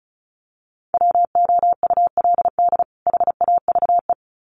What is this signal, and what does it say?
Morse code, heard as a keyed tone, about 34 wpm, 700 Hz. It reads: WOULD HAVE
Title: Sets of 2 Words 35wpm